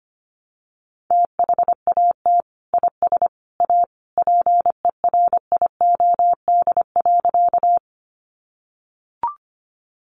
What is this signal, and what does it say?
Morse code, heard as a keyed tone, about 25 wpm, 700 Hz. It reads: THAT IS A PERIOD.
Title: Sentences from Top 800 Words 25wpm